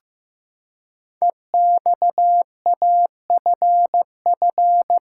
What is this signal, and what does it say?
Morse code, heard as a keyed tone, about 15 wpm, 700 Hz. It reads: EXAFF
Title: Code Groups - 5 Letters - 15wpm